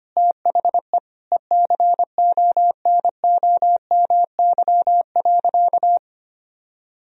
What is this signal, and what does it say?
Morse code, heard as a keyed tone, about 25 wpm, 700 Hz. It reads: THE ECONOMY.
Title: Sentences from Top 1000 Words - Review Part 3 - 25wpm